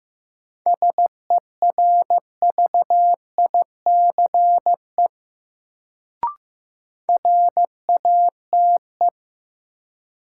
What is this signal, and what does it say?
Morse code, heard as a keyed tone, about 16 wpm, 700 Hz. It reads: SERVICE RATE
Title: Sentences from Top 700 Words - New Word Drill - 15wpm